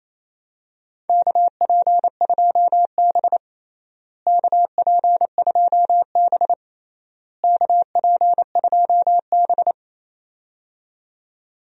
Morse code, heard as a keyed tone, about 28 wpm, 700 Hz. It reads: KP2B KP2B KP2B
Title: DXCC-100 Call Signs - 4 Characters - Repeated 3x - 28wpm